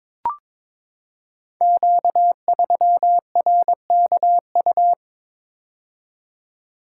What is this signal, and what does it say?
Morse code, heard as a keyed tone, about 22 wpm, 700 Hz. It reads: Q3RKU
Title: Code Groups - 5 Letter:Numbers - 22wpm